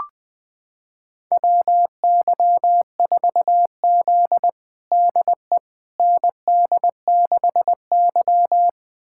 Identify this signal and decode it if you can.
Morse code, heard as a keyed tone, about 20 wpm, 700 Hz. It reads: WY4Z DE ND6Y
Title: USA Callsign DE Callsign - 4 Characters - 20wpm